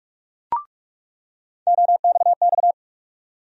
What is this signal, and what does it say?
Morse code, heard as a keyed tone, about 46 wpm, 700 Hz. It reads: OXX
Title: USA Call Sign Postfixes - 3 Letters 45wpm